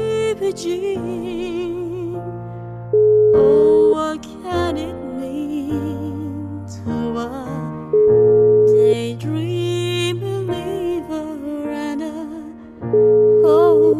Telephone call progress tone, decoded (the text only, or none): ringback tone